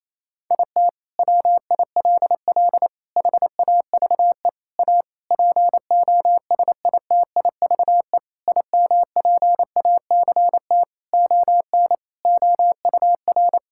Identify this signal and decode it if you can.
Morse code, heard as a keyed tone, about 27 wpm, 700 Hz. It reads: IT WILL HAVE A POSITIVE IMPACT ON OUR
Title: Sentences from Top 1000 Words - Review Part 6 - 28wpm